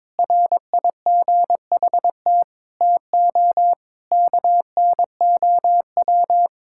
Morse code, heard as a keyed tone, about 22 wpm, 700 Hz. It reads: RIGHT TO KNOW
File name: practice / 22wpm / Sets of 7 Words - 22wpm